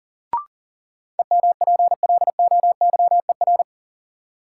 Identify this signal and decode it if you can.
Morse code, heard as a keyed tone, about 40 wpm, 700 Hz. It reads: EMPLOYER
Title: Sentences from Top 1600 Words - New Word Drill - 40wpm